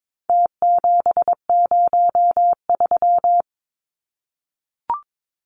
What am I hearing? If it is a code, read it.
Morse code, heard as a keyed tone, about 22 wpm, 700 Hz. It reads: T703
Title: Numbers - Contest Style - 4-Digit 22wpm